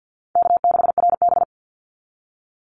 Morse code, heard as a keyed tone, about 50 wpm, 700 Hz. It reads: K6RB
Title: Easy Call Signs - 50wpm